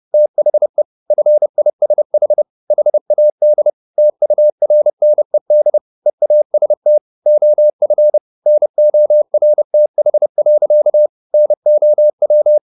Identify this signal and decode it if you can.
Morse code, heard as a keyed tone, about 30 wpm, 600 Hz. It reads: THE FISH HAD TURNED EAST OF NORTH. NOW
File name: the old man and the sea / the old man and the sea 1 30wpm